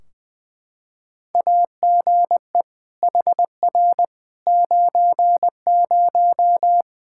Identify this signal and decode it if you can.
Morse code, heard as a keyed tone, about 20 wpm, 700 Hz. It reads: AGE HR 90
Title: Practice QSO - Exchange 3 - More Info - 20wpm